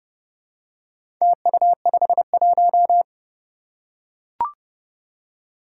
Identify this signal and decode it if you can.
Morse code, heard as a keyed tone, about 32 wpm, 700 Hz. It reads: TU51
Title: Four Challenging Letter-Number Combinations 30wpm